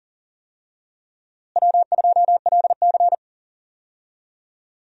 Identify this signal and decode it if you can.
Morse code, heard as a keyed tone, about 40 wpm, 700 Hz. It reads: W2LC
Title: USA Call Signs - 4 Characters - Repeated 2x - 40wpm